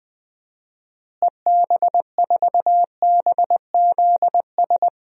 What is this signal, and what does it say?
Morse code, heard as a keyed tone, about 21 wpm, 700 Hz. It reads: EB4BZS
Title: DXCC-100 Call Signs - 6 Characters - Repeated 3x - 20wpm